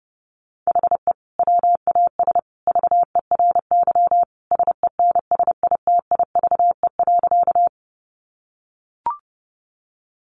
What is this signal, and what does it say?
Morse code, heard as a keyed tone, about 30 wpm, 700 Hz. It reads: HE WAS VERY SENSITIVE.
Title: Sentences from Top 1400 Words 30wpm